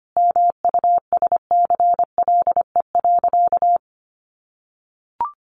Morse code, heard as a keyed tone, about 25 wpm, 700 Hz. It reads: MUSCLE.
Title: Sentences from Top 1300 Words 25wpm